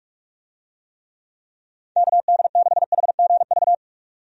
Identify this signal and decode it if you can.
Morse code, heard as a keyed tone, about 46 wpm, 700 Hz. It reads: KD6HGV
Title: USA Call Signs - 6 Characters - 45wpm